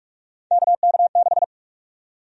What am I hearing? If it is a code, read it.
Morse code, heard as a keyed tone, about 46 wpm, 700 Hz. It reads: KK6
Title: USA Call Sign Prefixes - 45wpm